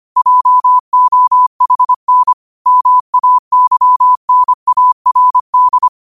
Morse code, heard as a keyed tone, about 25 wpm, 1000 Hz. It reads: JOHN MAYNARD
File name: qotdfiles / 25wpm